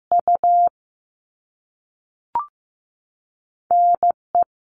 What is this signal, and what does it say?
Morse code, heard as a keyed tone, about 15 wpm, 700 Hz. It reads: U N E
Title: All Cut Numbers - 2-Digit 15wpm